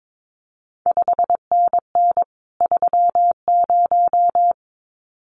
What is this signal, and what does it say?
Morse code, heard as a keyed tone, about 22 wpm, 700 Hz. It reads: 5NN 30
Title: CQ WW DX Contest Exchange Encore - 22wpm